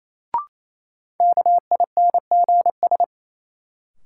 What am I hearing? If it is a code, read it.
Morse code, heard as a keyed tone, about 28 wpm, 700 Hz. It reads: KINGS
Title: Sentences from Top 1000 Words - Review Part 6 - New Word Drill - 28wpm